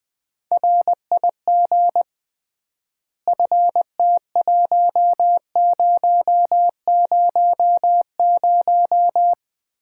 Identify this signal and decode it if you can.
Morse code, heard as a keyed tone, about 20 wpm, 700 Hz. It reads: RIG FT1000
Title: Common QSO Phrases - 3x Word-Spacing - 20wpm